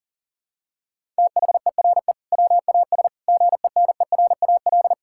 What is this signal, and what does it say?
Morse code, heard as a keyed tone, about 40 wpm, 700 Hz. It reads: THERE WAS GENERAL
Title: Sentences from Top 1600 Words 40wpm